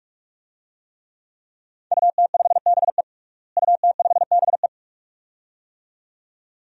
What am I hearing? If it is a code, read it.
Morse code, heard as a keyed tone, about 44 wpm, 700 Hz. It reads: UT5BE UT5BE
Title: DXCC-100 Call Signs - 5 Characters - Repeated 2x - 45wpm